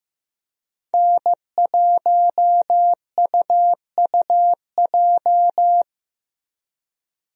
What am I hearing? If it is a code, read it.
Morse code, heard as a keyed tone, about 15 wpm, 700 Hz. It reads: N1UUJ
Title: USA Call Signs - 5 Characters - Repeated 3x - 15wpm